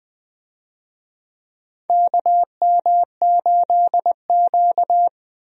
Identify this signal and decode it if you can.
Morse code, heard as a keyed tone, about 20 wpm, 700 Hz. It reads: KM8Q
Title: USA Call Signs - 4 Characters - Repeated 2x - 20wpm